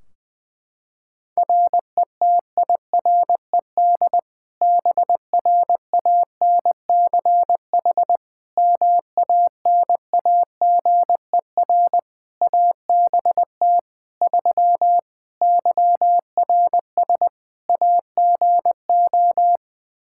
Morse code, heard as a keyed tone, about 20 wpm, 700 Hz. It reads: RETIRED BRANCH MANAGER ABT 3 YRS AGO
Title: Practice QSO - Exchange 3 - More Info - 20wpm